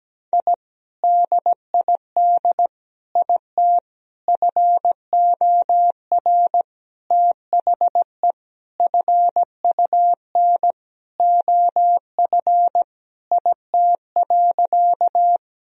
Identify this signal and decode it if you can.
Morse code, heard as a keyed tone, about 17 wpm, 700 Hz. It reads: I DID IT FOR THE FUN OF IT.
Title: Sentences from Top 1000 Words - Review Part 4 - 17wpm